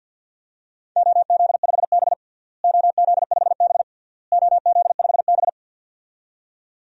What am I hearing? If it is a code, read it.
Morse code, heard as a keyed tone, about 52 wpm, 700 Hz. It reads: OZ5B OZ5B OZ5B
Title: DXCC-100 Call Signs - 4 Characters - Repeated 3x - 50wpm